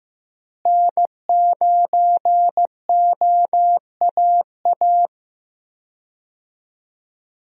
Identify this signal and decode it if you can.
Morse code, heard as a keyed tone, about 15 wpm, 700 Hz. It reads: N9OAA
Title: USA Call Signs - 5 Characters - Repeated 2x - 15wpm